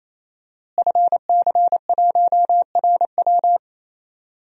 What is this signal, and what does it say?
Morse code, heard as a keyed tone, about 28 wpm, 700 Hz. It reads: FC1RW
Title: DXCC-100 Call Signs - 5 Characters - Repeated 3x - 28wpm